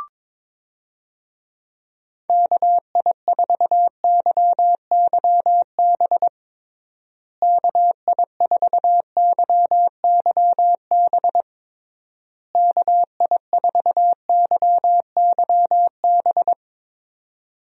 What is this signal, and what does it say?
Morse code, heard as a keyed tone, about 22 wpm, 700 Hz. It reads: KI4YYB KI4YYB KI4YYB
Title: USA Call Signs - 6 Characters - Repeated 3x - 22wpm